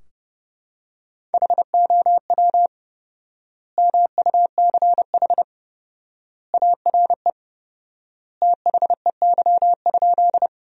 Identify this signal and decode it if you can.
Morse code, heard as a keyed tone, about 30 wpm, 700 Hz. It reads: HOW MUCH ARE THEY?
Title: 4x Word-Spacing - Sentences from Top 200 Words - 30wpm